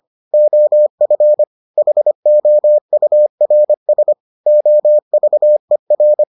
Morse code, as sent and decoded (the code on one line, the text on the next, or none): --- ..-. / .... --- ..- .-. ... / --- ...- . .-.
OF HOURS OVER